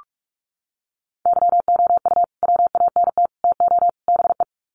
Morse code, heard as a keyed tone, about 44 wpm, 700 Hz. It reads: YOU WANT TO BE
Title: Sets of 4 Words 45wpm